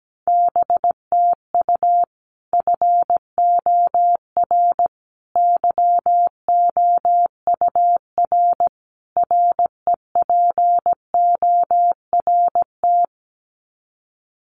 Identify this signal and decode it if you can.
Morse code, heard as a keyed tone, about 17 wpm, 700 Hz. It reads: BTU FOR YOUR REPORT
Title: Common QSO Phrases 17wpm